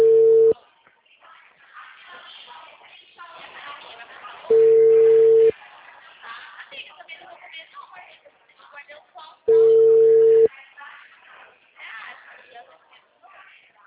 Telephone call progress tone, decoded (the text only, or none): ringback tone